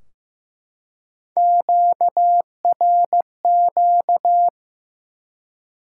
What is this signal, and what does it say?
Morse code, heard as a keyed tone, about 15 wpm, 700 Hz. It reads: QRQ
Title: Top 90 QSO Elements 15wpm